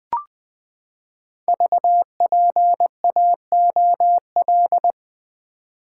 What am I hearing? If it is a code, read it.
Morse code, heard as a keyed tone, about 20 wpm, 700 Hz. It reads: VPAOL
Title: Code Groups - 5 Letters - 20wpm